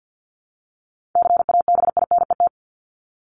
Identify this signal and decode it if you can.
Morse code, heard as a keyed tone, about 50 wpm, 700 Hz. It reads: CABINET
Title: Top 1751-2000 Words 50wpm